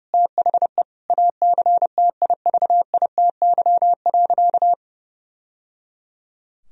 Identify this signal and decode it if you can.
Morse code, heard as a keyed tone, about 30 wpm, 700 Hz. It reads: THE ACTIVITY.
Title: Sentences from Top 1000 Words - Review Part 4 - 30wpm